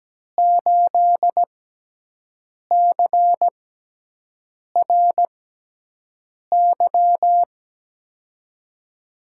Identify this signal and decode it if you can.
Morse code, heard as a keyed tone, about 17 wpm, spaced as for 6 wpm, 700 Hz. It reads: 8CRY
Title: USA Call Signs - 5 Characters - 6x Character Spacing - 17wpm